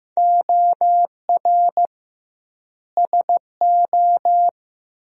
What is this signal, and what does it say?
Morse code, heard as a keyed tone, about 15 wpm, 700 Hz. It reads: OR SO
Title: 2x Word-Spacing - Sentences from Top 100 Words - 15wpm